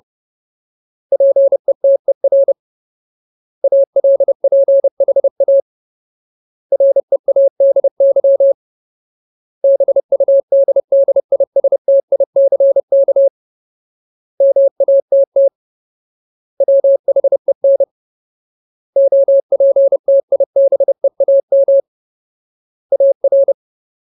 Morse code, heard as a keyed tone, about 30 wpm, 550 Hz. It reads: PETER ALPHA READY BUDDISTICK MATT WHEN OPTIBEAM AR
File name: Ultimate Word List / 30WPM Ultimate